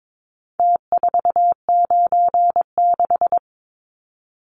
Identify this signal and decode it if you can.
Morse code, heard as a keyed tone, about 22 wpm, 700 Hz. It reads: T496
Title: Numbers - Contest Style - 4-Digit 22wpm